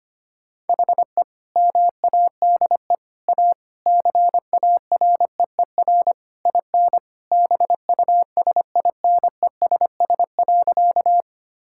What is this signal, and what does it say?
Morse code, heard as a keyed tone, about 25 wpm, 700 Hz. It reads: HE MADE A CAREER IN BUSINESS.
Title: Sentences from Top 1000 Words - Review Part 6 - 25wpm